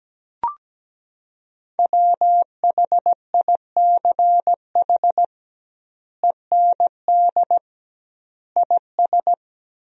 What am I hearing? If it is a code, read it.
Morse code, heard as a keyed tone, about 17 wpm, 700 Hz. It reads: WHICH END IS
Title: 2x Word-Spacing - Sentences from Top 300 Words - 17wpm